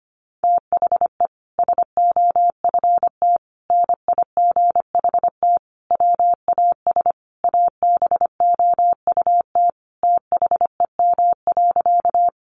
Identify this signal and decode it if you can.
Morse code, heard as a keyed tone, about 25 wpm, 700 Hz. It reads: THE SOFT NIGHT WAS ABOUT THEM.
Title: Sentences from Top 1500 Words 25wpm